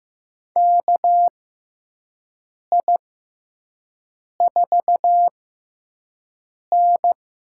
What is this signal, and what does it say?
Morse code, heard as a keyed tone, about 15 wpm, spaced as for 5 wpm, 700 Hz. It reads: KI4N
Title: USA Call Signs - 6 Characters - 6x Character Spacing - 15wpm